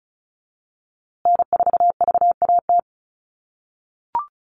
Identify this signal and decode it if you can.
Morse code, heard as a keyed tone, about 34 wpm, 700 Hz. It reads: N4VAT
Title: USA Call Signs Repeated 3x 35wpm